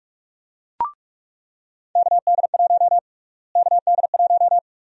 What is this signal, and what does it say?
Morse code, heard as a keyed tone, about 44 wpm, 700 Hz. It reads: KD1 KD1
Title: USA Call Sign Prefixes - Repeated 2x - 45wpm